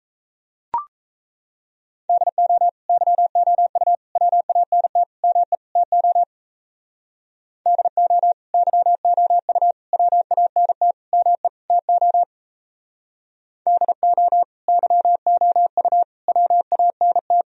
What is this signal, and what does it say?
Morse code, heard as a keyed tone, about 35 wpm, 700 Hz. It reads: DO YOU WANT ME TO DO YOU WANT ME TO DO YOU WANT
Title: Speed-Racing - Sets of 5 Words 28wpm